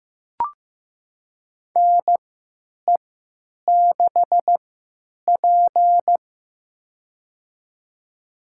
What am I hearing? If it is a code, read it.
Morse code, heard as a keyed tone, about 16 wpm, spaced as for 9 wpm, 700 Hz. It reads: NE6P
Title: USA Call Signs - 4 Characters - 3x Character Spacing - 15wpm